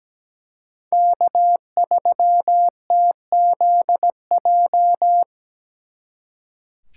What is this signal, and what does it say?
Morse code, heard as a keyed tone, about 20 wpm, 700 Hz. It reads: K3TZJ
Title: Speed Racing - USA 1x3 Call Signs 17wpm